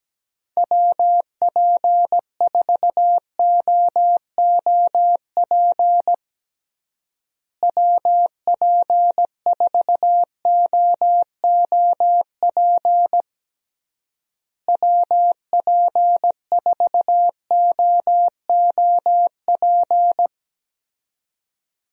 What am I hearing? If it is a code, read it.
Morse code, heard as a keyed tone, about 17 wpm, 700 Hz. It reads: WP4OOP WP4OOP WP4OOP
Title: USA Call Signs - 6 Characters - Repeated 3x - 17wpm